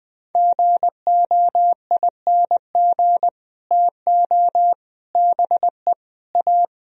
Morse code, heard as a keyed tone, about 21 wpm, 700 Hz. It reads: GOING TO BE A
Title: Speed-Racing - Sets of 4 Words 20wpm